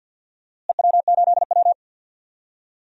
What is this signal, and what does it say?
Morse code, heard as a keyed tone, about 52 wpm, 700 Hz. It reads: EW8W
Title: DXCC-100 Call Signs - 4 Characters - Repeated 3x - 50wpm